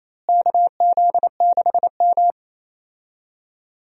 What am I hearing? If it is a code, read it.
Morse code, heard as a keyed tone, about 28 wpm, 700 Hz. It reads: KZ6M
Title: USA Call Signs - 4 Characters - Repeated 2x - 28wpm